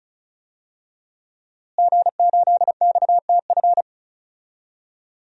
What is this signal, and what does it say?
Morse code, heard as a keyed tone, about 36 wpm, 700 Hz. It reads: G8XTF
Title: DXCC-100 Call Signs - 5 Characters - Repeated 3x - 35wpm